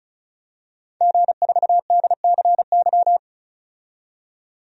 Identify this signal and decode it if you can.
Morse code, heard as a keyed tone, about 36 wpm, 700 Hz. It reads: G4DCY
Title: DXCC-100 Call Signs - 5 Characters - 35wpm